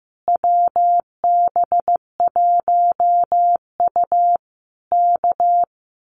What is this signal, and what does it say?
Morse code, heard as a keyed tone, about 15 wpm, 700 Hz. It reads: WB1U K
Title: Common QSO Phrases Encore 15wpm